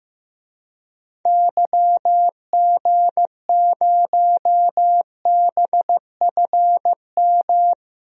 Morse code, heard as a keyed tone, about 15 wpm, 700 Hz. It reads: YG0BFM
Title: DXCC-100 Call Signs - 6 Characters - Repeated 2x - 15wpm